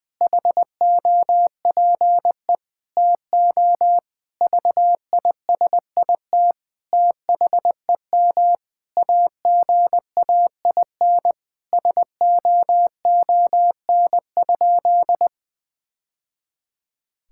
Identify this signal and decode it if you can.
Morse code, heard as a keyed tone, about 20 wpm, 700 Hz. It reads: HOPE TO VISIT THEM AGAIN SOON?
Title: Sentences from Top 1000 Words - Review Part 6 - 20wpm